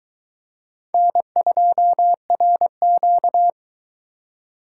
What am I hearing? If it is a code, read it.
Morse code, heard as a keyed tone, about 22 wpm, 700 Hz. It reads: N2RQ